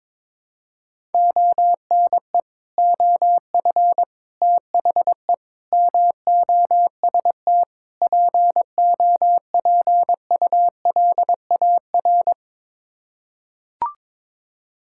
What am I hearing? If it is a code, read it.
Morse code, heard as a keyed tone, about 21 wpm, 700 Hz. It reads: ONE OF THE MOST POPULAR
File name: practice / 22wpm / Build A Phrase - 5-Word Phrases - 22wpm